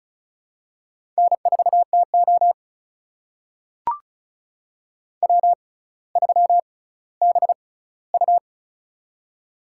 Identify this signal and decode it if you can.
Morse code, heard as a keyed tone, about 36 wpm, spaced as for 18 wpm, 700 Hz. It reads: N4TO W3BU
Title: USA Call Signs - 4 Characters - 6x Character Spacing - 35wpm